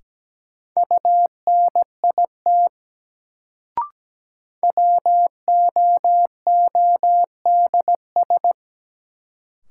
Morse code, heard as a keyed tone, about 17 wpm, 700 Hz. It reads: UNIT WOODS
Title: Sentences from Top 1000 Words - Review Part 2 - New Word Drill - 17wpm